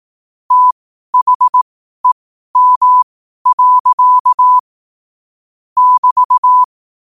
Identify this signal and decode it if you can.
Morse code, heard as a keyed tone, about 18 wpm, spaced as for 12 wpm, 1000 Hz. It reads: THEM. <BT>
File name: qotdfiles / 12wpm